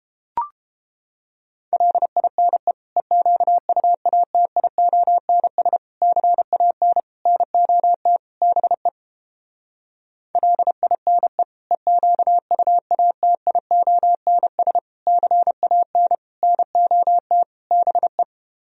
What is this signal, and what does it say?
Morse code, heard as a keyed tone, about 29 wpm, 700 Hz. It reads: LINE EQUATIONS CAN NOT BE LINE EQUATIONS CAN NOT BE